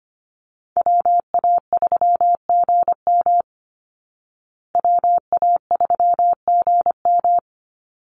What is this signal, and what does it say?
Morse code, heard as a keyed tone, about 25 wpm, 700 Hz. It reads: WA3GM WA3GM
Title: Call Signs Repeated 2x - 25wpm